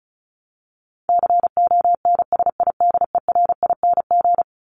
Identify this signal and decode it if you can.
Morse code, heard as a keyed tone, about 34 wpm, 700 Hz. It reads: CONSIDERING
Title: ING Suffix - Top 100 Words with Base Words 35wpm